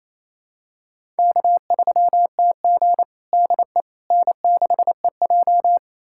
Morse code, heard as a keyed tone, about 28 wpm, 700 Hz. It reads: K3TG DE N6EJ
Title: USA Callsign DE Callsign - 4 Characters - 28wpm